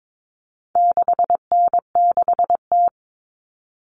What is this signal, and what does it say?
Morse code, heard as a keyed tone, about 22 wpm, 700 Hz. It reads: 6N6T
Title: All Cut Numbers - 4-Digit 22wpm